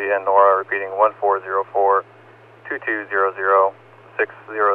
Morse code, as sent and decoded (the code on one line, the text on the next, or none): none